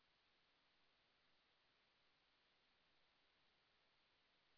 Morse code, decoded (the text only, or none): none